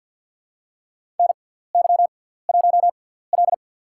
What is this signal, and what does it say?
Morse code, heard as a keyed tone, about 53 wpm, spaced as for 17 wpm, 700 Hz. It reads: NY1L
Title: USA Call Signs - 4 Characters - 6x Character Spacing - 50wpm